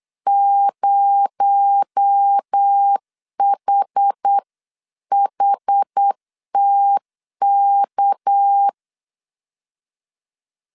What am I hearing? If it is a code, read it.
Morse code, heard as a keyed tone, about 8 wpm, 800 Hz. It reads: 0HHTK
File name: 8wpm